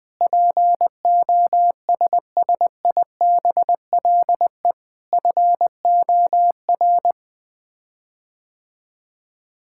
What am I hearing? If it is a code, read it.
Morse code, heard as a keyed tone, about 20 wpm, 700 Hz. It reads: POSSIBLE FOR
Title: Sets of 8 Words - 20wpm